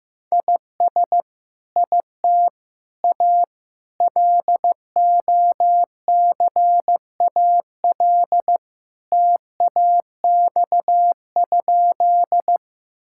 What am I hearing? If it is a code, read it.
Morse code, heard as a keyed tone, about 15 wpm, 700 Hz. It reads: IS IT A LOCAL TAX?
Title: Sentences from Top 1000 Words - Review Part 5 - 15wpm